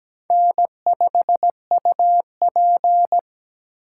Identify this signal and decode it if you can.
Morse code, heard as a keyed tone, about 17 wpm, 700 Hz. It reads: N5UP
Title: USA Call Signs - 4 Characters - 3x Character Spacing - 17wpm